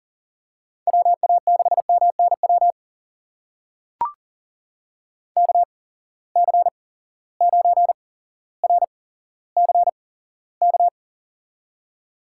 Morse code, heard as a keyed tone, about 39 wpm, spaced as for 17 wpm, 700 Hz. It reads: WA6MNW KC9RCK
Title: USA Call Signs - 6 Characters - 8x Character Spacing - 40wpm